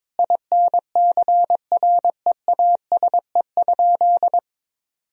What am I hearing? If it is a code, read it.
Morse code, heard as a keyed tone, about 22 wpm, 700 Hz. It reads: INCREASE?
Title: Sentences from Top 1000 Words - Review Part 4 - 22wpm